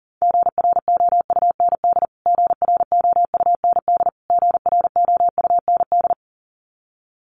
Morse code, heard as a keyed tone, about 37 wpm, 700 Hz. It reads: GROUND GROUND GROUND
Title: Top 500 Words Repeated 40wpm